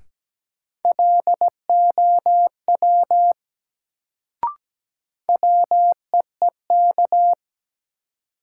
Morse code, heard as a keyed tone, about 17 wpm, 700 Hz. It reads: LOW WEEK
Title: Sentences from Top 500 Words - New Word Drill - 17wpm